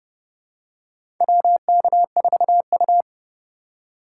Morse code, heard as a keyed tone, about 31 wpm, 700 Hz. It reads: WK4U
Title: USA Call Signs - 2x Character Spacing - 30wpm